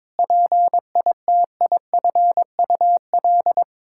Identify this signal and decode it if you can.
Morse code, heard as a keyed tone, about 22 wpm, 700 Hz. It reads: PITIFUL
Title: Pangram Challenge v2 - 22wpm